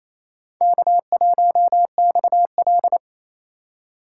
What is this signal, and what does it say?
Morse code, heard as a keyed tone, about 28 wpm, 700 Hz. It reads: K1XL
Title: USA Call Signs - 4 Characters - 3x Character Spacing - 28wpm